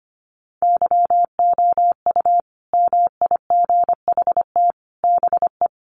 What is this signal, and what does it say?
Morse code, heard as a keyed tone, about 25 wpm, 700 Hz. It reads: YOU MIGHT BE
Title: Sets of 5 Words Encore2 25wpm